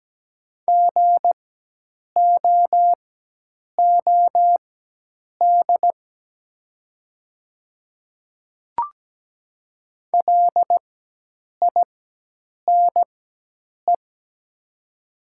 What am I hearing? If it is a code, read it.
Morse code, heard as a keyed tone, about 17 wpm, spaced as for 8 wpm, 700 Hz. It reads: GOOD LINE
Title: 4x Character Spacing - Top 200 Words - 17wpm